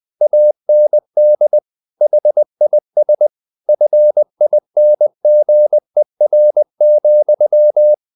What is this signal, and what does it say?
Morse code, heard as a keyed tone, about 20 wpm, 600 Hz. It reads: AND HIS FINGER,
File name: the old man and the sea / the old man and the sea part 1 20wpm